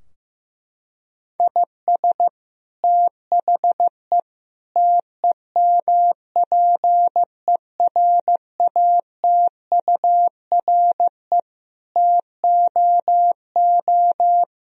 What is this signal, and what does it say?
Morse code, heard as a keyed tone, about 15 wpm, 700 Hz. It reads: IS THE TEMPERATURE TOO
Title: Sentences from Top 1000 Words - Review Part 4 - 15wpm